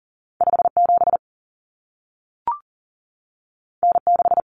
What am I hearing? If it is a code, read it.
Morse code, heard as a keyed tone, about 40 wpm, 700 Hz. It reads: 57 N6
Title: Common Cut Numbers - 2-Digit 40wpm